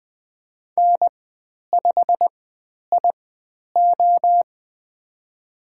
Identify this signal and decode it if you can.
Morse code, heard as a keyed tone, about 20 wpm, spaced as for 10 wpm, 700 Hz. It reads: N5IO
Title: 102 Lesson 17g - Letter u - Call Signs 20wpm